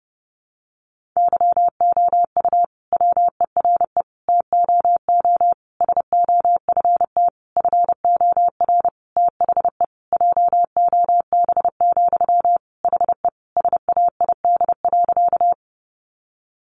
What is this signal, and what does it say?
Morse code, heard as a keyed tone, about 30 wpm, 700 Hz. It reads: YOU WERE TOO SOFT FOR THE JOB, HE SAID.
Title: Sentences from Top 1500 Words 30wpm